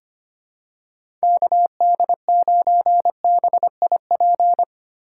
Morse code, heard as a keyed tone, about 25 wpm, 700 Hz. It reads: KD9BIP
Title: USA Call Signs - 6 Characters - 2x Character Spacing - 25wpm